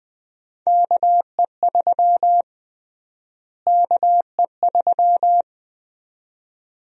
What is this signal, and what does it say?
Morse code, heard as a keyed tone, about 20 wpm, 700 Hz. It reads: KE3 KE3
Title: USA Call Sign Prefixes - Repeated 2x - 20wpm